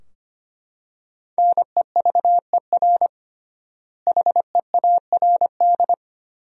Morse code, heard as a keyed tone, about 25 wpm, 700 Hz. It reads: NEVER HEARD
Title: 3x Word-Spacing - Sentences from Top 400 Words - 25wpm